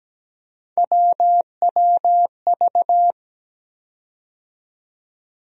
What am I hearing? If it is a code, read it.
Morse code, heard as a keyed tone, about 17 wpm, 700 Hz. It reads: WWV
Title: USA Call Sign Postfixes - 3 Letters 17wpm